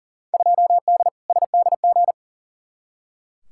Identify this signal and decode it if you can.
Morse code, heard as a keyed tone, about 39 wpm, 700 Hz. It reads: 2D SDG
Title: Field Day Exchanges Encore - 40wpm